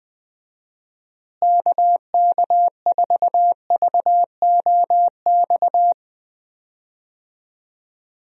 Morse code, heard as a keyed tone, about 20 wpm, 700 Hz. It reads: KK4VOX
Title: USA Call Signs - 6 Characters - Repeated 2x - 20wpm